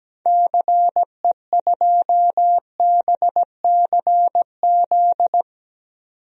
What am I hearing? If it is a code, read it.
Morse code, heard as a keyed tone, about 17 wpm, 700 Hz. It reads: CE2BCZ
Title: DXCC-100 Call Signs - 6 Characters - 17wpm